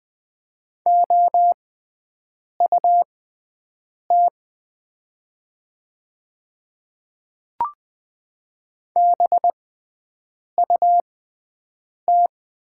Morse code, heard as a keyed tone, about 20 wpm, spaced as for 7 wpm, 700 Hz. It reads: OUT BUT
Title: 6x Character Spacing - Top 100 Words - 20wpm